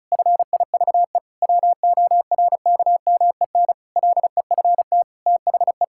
Morse code, heard as a keyed tone, about 36 wpm, 700 Hz. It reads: FIVE WORKMEN LEFT THE
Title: Pangram Challenge v2 - 35wpm